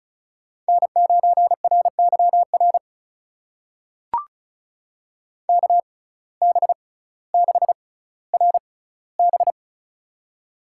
Morse code, heard as a keyed tone, about 36 wpm, spaced as for 18 wpm, 700 Hz. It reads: N9RYR KB6RB
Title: USA Call Signs - 6x Character Spacing - 35wpm